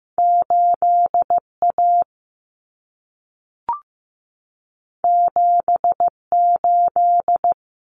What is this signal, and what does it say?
Morse code, heard as a keyed tone, about 15 wpm, 700 Hz. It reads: 8A 78
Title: Common Cut Numbers - 2-Digit 15wpm